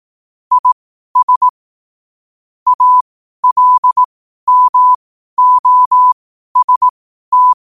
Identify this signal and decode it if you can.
Morse code, heard as a keyed tone, about 18 wpm, spaced as for 12 wpm, 1000 Hz. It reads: IS ALMOST
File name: qotdfiles / 12wpm